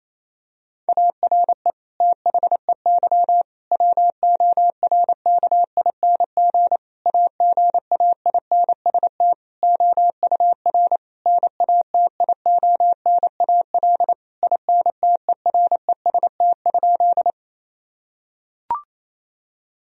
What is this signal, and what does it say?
Morse code, heard as a keyed tone, about 28 wpm, 700 Hz. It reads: ARE THEY WORKING AGAINST OUR NATIONAL INTEREST?
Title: Sentences from Top 1000 Words - Review Part 4 - 28wpm